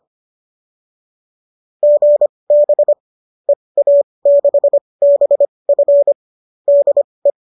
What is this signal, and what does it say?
Morse code, heard as a keyed tone, about 25 wpm, spaced as for 20 wpm, 570 Hz. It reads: GB EA6BF DE